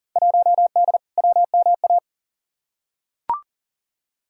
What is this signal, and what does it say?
Morse code, heard as a keyed tone, about 40 wpm, 700 Hz. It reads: 1D WMA
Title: Field Day Exchanges - 40wpm